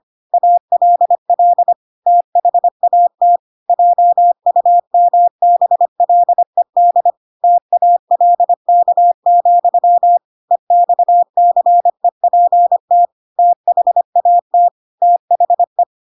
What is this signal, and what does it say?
Morse code, heard as a keyed tone, about 25 wpm, 700 Hz. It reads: ALL THAT JUMBLED TALK, EXCEPT THAT THE
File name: Pinocchio / Pinocchio ch7 25 25 ik1hge